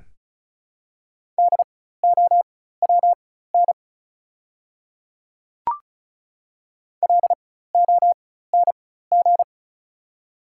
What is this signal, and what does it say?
Morse code, heard as a keyed tone, about 35 wpm, spaced as for 16 wpm, 700 Hz. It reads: DOWN LONG
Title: 4x Character Spacing - Top 100 Words - 35wpm